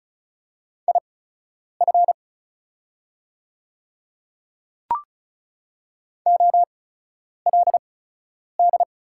Morse code, heard as a keyed tone, about 36 wpm, spaced as for 10 wpm, 700 Hz. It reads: IF OLD
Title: 8x Character Spacing - Top 200 Words - 35wpm